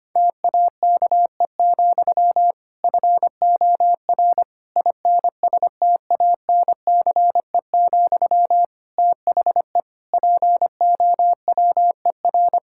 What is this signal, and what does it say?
Morse code, heard as a keyed tone, about 25 wpm, 700 Hz. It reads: TAKE, FOR INSTANCE, THE POWER
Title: Sentences from Top 1000 Words - Review Part 5 - 25wpm